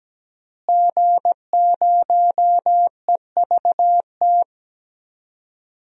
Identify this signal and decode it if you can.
Morse code, heard as a keyed tone, about 17 wpm, 700 Hz. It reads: G0EVT
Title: DXCC-100 Call Signs - 5 Characters - Repeated 3x - 17wpm